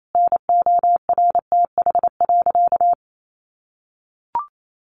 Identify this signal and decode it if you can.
Morse code, heard as a keyed tone, about 28 wpm, 700 Hz. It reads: NORTH.
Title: Sentences from Top 1200 Words 28wpm